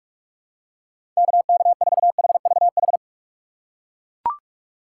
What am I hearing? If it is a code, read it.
Morse code, heard as a keyed tone, about 41 wpm, 700 Hz. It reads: KK4HVH
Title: USA Call Signs - 6 Characters - 2x Character Spacing - 45wpm